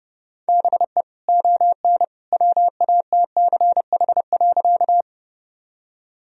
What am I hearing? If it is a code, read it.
Morse code, heard as a keyed tone, about 30 wpm, 700 Hz. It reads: BE ON WATCH.
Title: Sentences from Top 1000 Words - Review Part 1 - 30wpm